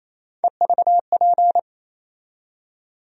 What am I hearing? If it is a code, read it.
Morse code, heard as a keyed tone, about 28 wpm, 700 Hz. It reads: EVP
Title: USA Call Sign Postfixes - 3 Letters 28wpm